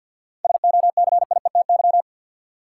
Morse code, heard as a keyed tone, about 50 wpm, 700 Hz. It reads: SOCIETY
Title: Sentences from Top 700 Words - New Word Drill - 50wpm